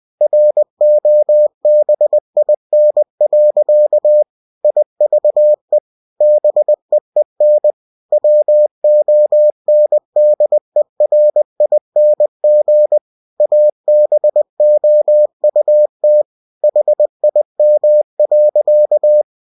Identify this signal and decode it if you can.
Morse code, heard as a keyed tone, about 20 wpm, 600 Hz. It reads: ROBIN. IVE BEEN WONDERING ABOUT HIM.
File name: cw-books / winnie-the-pooh / winnie-20wpm-page-0097